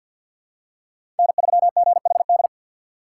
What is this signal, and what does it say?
Morse code, heard as a keyed tone, about 48 wpm, 690 Hz. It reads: N3GHD
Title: USA Call Signs - 5 Characters - 3x Character Spacing - 50wpm